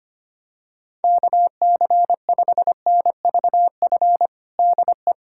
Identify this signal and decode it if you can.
Morse code, heard as a keyed tone, about 25 wpm, 700 Hz. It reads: KC5NVF DE
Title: USA Callsign DE Callsign 25wpm